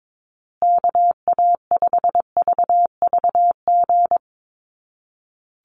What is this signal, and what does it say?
Morse code, heard as a keyed tone, about 22 wpm, 700 Hz. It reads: KA5VVG
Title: USA Call Signs Repeated 3x 22wpm